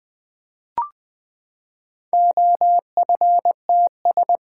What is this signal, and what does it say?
Morse code, heard as a keyed tone, about 20 wpm, 700 Hz. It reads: OFTS
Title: Four Letter-Number Combinations - 20wpm